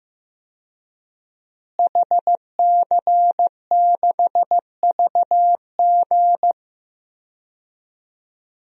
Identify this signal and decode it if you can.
Morse code, heard as a keyed tone, about 16 wpm, 700 Hz. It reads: HC6VG
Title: DXCC-100 Call Signs - 5 Characters - Repeated 3x - 15wpm